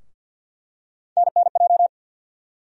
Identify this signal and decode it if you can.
Morse code, heard as a keyed tone, about 47 wpm, 700 Hz. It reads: NNJ
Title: USA Call Sign Postfixes - 3 Letters 50wpm